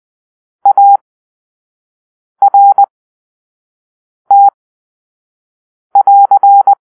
Morse code, heard as a keyed tone, about 20 wpm, spaced as for 5 wpm, 800 Hz. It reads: ART<AR>
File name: arrl / Grp-1 Drill-1